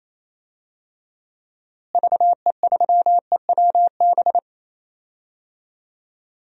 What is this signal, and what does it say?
Morse code, heard as a keyed tone, about 28 wpm, 700 Hz. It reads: VE3EWB
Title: DXCC-100 Call Signs - 6 Characters - Repeated 2x - 28wpm